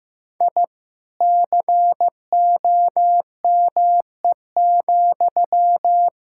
Morse code, heard as a keyed tone, about 14 wpm, 700 Hz. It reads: I COME,
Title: Sentences from Top 1000 Words - Review Part 5 - 15wpm